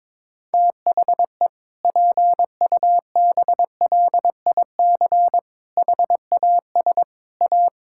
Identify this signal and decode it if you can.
Morse code, heard as a keyed tone, about 22 wpm, 700 Hz. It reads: THE PUBLIC HAS A
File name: practice / 22wpm / Sets of 7 Words - 22wpm